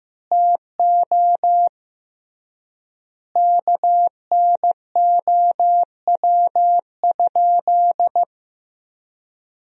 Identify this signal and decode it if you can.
Morse code, heard as a keyed tone, about 15 wpm, 700 Hz. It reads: TO KNOW?
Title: 3x Word-Spacing - Sentences from Top 200 Words - 15wpm